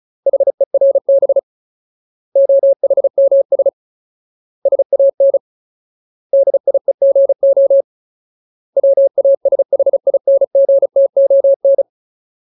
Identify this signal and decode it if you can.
Morse code, heard as a keyed tone, about 34 wpm, 550 Hz. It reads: HERB OHMS SAN DIEGO WASHINGTON
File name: Ultimate Word List / 35WPM Ultimate